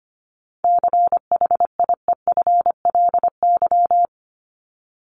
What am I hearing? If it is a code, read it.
Morse code, heard as a keyed tone, about 24 wpm, 700 Hz. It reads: CHIEFLY
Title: LY Suffix - Top 100 Words with Base Words 25wpm